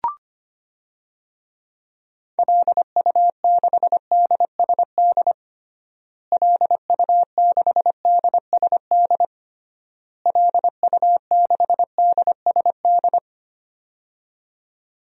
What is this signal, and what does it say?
Morse code, heard as a keyed tone, about 25 wpm, 690 Hz. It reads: LU6DSD LU6DSD LU6DSD
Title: DXCC-100 Call Signs - 6 Characters - Repeated 3x - 25wpm